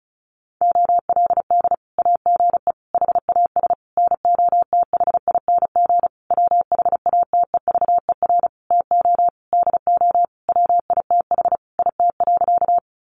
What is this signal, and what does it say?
Morse code, heard as a keyed tone, about 34 wpm, 700 Hz. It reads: OLD AGE HAS NOTHING WHATEVER TO DO WITH IT.
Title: Sentences from Top 1100 Words 35wpm